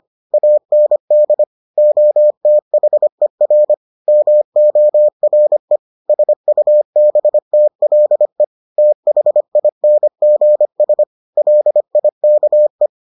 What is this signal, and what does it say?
Morse code, heard as a keyed tone, about 25 wpm, 600 Hz. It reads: AND OTHER MORE SUBTLE THINGS LIKE